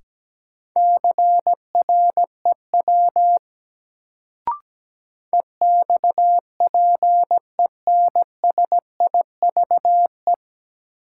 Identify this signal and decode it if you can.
Morse code, heard as a keyed tone, about 17 wpm, 700 Hz. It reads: CREW EXPENSIVE
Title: Sentences from Top 1500 Words - New Word Drill Alt - 17wpm